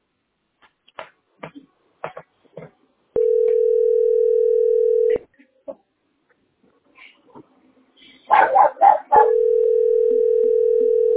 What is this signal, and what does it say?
Ringback tone (North America), 440 + 480 Hz.